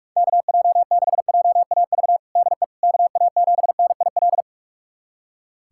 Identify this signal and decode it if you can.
Morse code, heard as a keyed tone, about 46 wpm, 700 Hz. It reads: KJ6JAV DE KA7NIL
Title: USA Callsign DE Callsign 45wpm